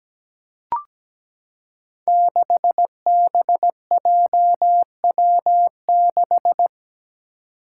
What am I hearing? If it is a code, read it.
Morse code, heard as a keyed tone, about 17 wpm, 700 Hz. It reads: 6BJW6
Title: Five Challenging Letter-Number Combinations 17wpm